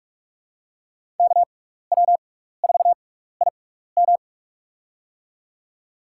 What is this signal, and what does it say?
Morse code, heard as a keyed tone, about 44 wpm, spaced as for 15 wpm, 690 Hz. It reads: KW4IM
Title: USA Call Signs - 5 Characters - 6x Character Spacing - 45wpm